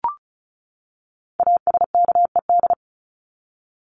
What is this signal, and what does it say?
Morse code, heard as a keyed tone, about 34 wpm, 700 Hz. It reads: ASKED
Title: ED Suffix - Top 100 Words with Base Words 35wpm